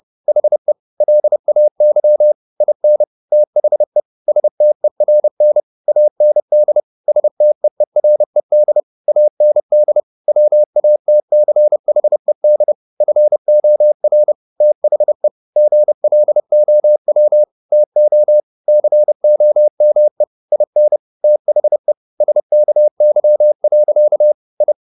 Morse code, heard as a keyed tone, about 30 wpm, 600 Hz. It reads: HE LAY IN THE STERN AND STEERED AND WATCHED FOR THE GLOW TO COME IN THE SKY. I